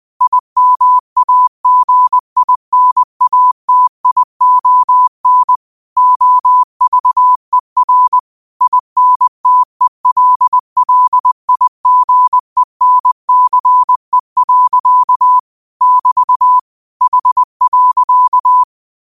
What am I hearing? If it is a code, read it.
Morse code, heard as a keyed tone, about 20 wpm, 1000 Hz. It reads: IMAGINATION OVER INTELLIGENCE. <BT> H.
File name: qotdfiles / 20wpm